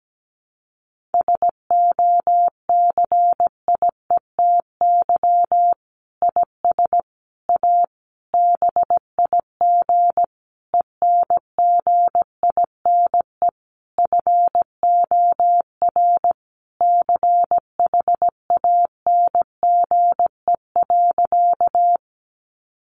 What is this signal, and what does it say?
Morse code, heard as a keyed tone, about 17 wpm, 700 Hz. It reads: SOCIETY IS A BIG ENGINE FOR CHANGE.
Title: Sentences from Top 1500 Words 17wpm